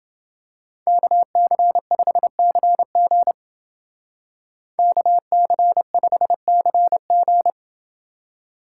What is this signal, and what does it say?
Morse code, heard as a keyed tone, about 28 wpm, 700 Hz. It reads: KC5CG KC5CG